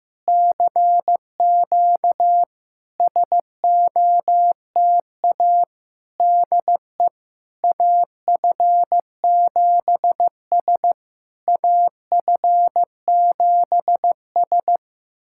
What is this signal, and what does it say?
Morse code, heard as a keyed tone, about 15 wpm, 700 Hz. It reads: CQ SOTA DE AF7S AF7S
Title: SOTA QSO Practice 15wpm